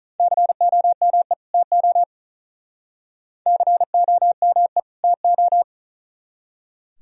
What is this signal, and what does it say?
Morse code, heard as a keyed tone, about 45 wpm, 700 Hz. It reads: COME TO COME TO